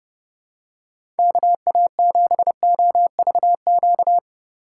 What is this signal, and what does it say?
Morse code, heard as a keyed tone, about 30 wpm, 700 Hz. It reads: KA7OVQ
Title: USA Call Signs - 6 Characters - 4x Character Spacing - 30wpm